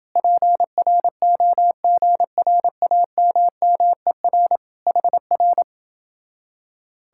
Morse code, heard as a keyed tone, about 29 wpm, 700 Hz. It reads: PROGRAMMER HR